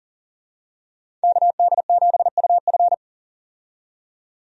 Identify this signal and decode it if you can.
Morse code, heard as a keyed tone, about 40 wpm, 700 Hz. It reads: KD7UF
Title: USA Call Signs - 5 Characters - 40wpm